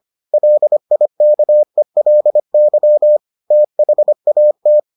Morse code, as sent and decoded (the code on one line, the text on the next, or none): .-.. .. -.- . .-.. -.-- / - .... .- -
LIKELY THAT